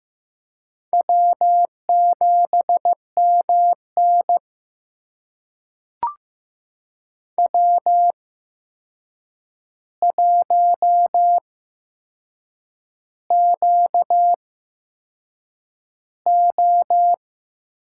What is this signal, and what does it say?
Morse code, heard as a keyed tone, about 15 wpm, spaced as for 6 wpm, 700 Hz. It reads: W7MN W1QO
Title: USA Call Signs - 4 Characters - 8x Character Spacing - 15wpm